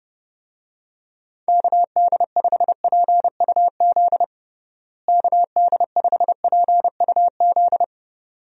Morse code, heard as a keyed tone, about 32 wpm, 700 Hz. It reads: KD5PUZ KD5PUZ
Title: USA Call Signs - 6 Characters - Repeated 2x - 30wpm